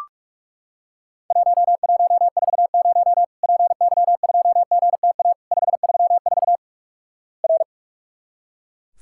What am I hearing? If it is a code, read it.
Morse code, heard as a keyed tone, about 44 wpm, 700 Hz. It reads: 1140 PY2GTA 534 R
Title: Worked All Europe DX Contest - QTC Exchange - 45wpm